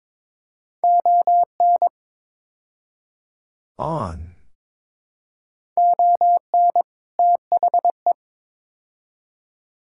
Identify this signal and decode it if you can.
Morse code, heard as a keyed tone, about 22 wpm, 700 Hz. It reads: ON ON THE
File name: practice / 22wpm / Build A Phrase - 5-Word Phrases - 22wpm